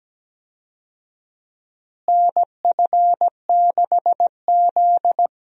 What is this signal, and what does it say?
Morse code, heard as a keyed tone, about 17 wpm, 700 Hz. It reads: NF6Z
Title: USA Call Signs - 4 Characters - Repeated 3x - 17wpm